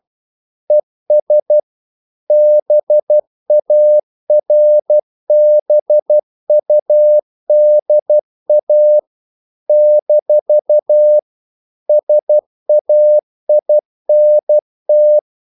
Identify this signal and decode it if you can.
Morse code, heard as a keyed tone, about 12 wpm, 600 Hz. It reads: ES BARBUDA - SAINT
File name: CW / Practice / NationalCapitals12